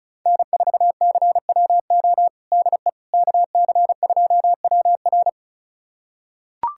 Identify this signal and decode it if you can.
Morse code, heard as a keyed tone, about 36 wpm, 700 Hz. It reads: N4CWO DE KC2WR
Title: USA Callsign DE Callsign - 5 Characters - 35wpm